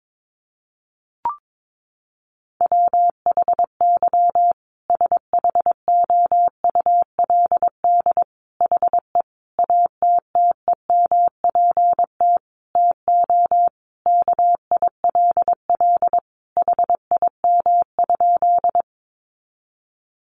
Morse code, heard as a keyed tone, about 22 wpm, 700 Hz. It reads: WHY SHOULD HE ATTEMPT TO KILL HIM?
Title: Sentences from Top 1400 Words 22wpm